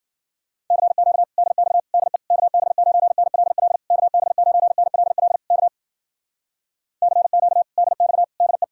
Morse code, heard as a keyed tone, about 52 wpm, 700 Hz. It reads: CQ DX DE KB9NLL KB9NLL K CQ DX DE